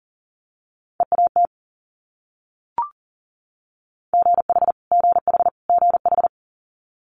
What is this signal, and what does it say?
Morse code, heard as a keyed tone, about 38 wpm, 700 Hz. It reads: EAT GH GH GH
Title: Multi-letter Phonemes Repeated 3x 40wpm